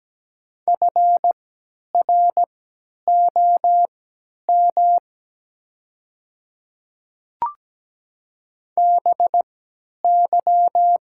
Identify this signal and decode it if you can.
Morse code, heard as a keyed tone, about 17 wpm, spaced as for 10 wpm, 700 Hz. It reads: FROM BY
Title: 3x Character Spacing - Top 100 Words - 17wpm